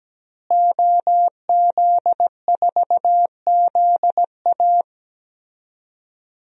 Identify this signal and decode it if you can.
Morse code, heard as a keyed tone, about 17 wpm, 700 Hz. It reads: OZ4ZA
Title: DXCC-100 Call Signs - 5 Characters - Repeated 2x - 17wpm